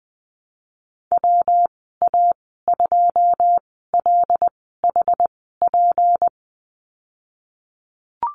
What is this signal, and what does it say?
Morse code, heard as a keyed tone, about 19 wpm, spaced as for 14 wpm, 700 Hz. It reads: WA2LHP
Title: 167 Lesson 27g - Number 7 - Call Signs 20wpm